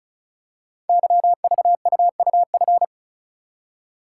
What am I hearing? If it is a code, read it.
Morse code, heard as a keyed tone, about 36 wpm, 700 Hz. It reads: YVUUF
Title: Code Groups - 5 Letter:Numbers - 35wpm